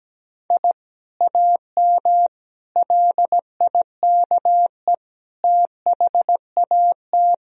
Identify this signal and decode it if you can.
Morse code, heard as a keyed tone, about 17 wpm, 700 Hz. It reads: I AM LIKE THAT
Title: Sentences from Top 1000 Words - Review Part 3 - 17wpm